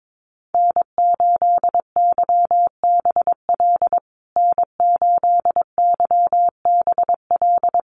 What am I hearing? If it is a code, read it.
Morse code, heard as a keyed tone, about 22 wpm, 690 Hz. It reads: N8YBL N8YBL
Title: USA Call Signs Repeated 3x 22wpm